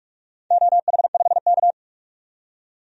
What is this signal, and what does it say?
Morse code, heard as a keyed tone, about 46 wpm, 700 Hz. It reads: OH5K
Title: DXCC-100 Call Signs - 4 Characters - 45wpm